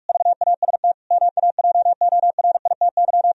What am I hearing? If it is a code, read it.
Morse code, heard as a keyed tone, about 62 wpm, spaced as for 53 wpm, 700 Hz. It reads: VAST MAJORITY
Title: Speed-Racing - Sets of 4 Words 45wpm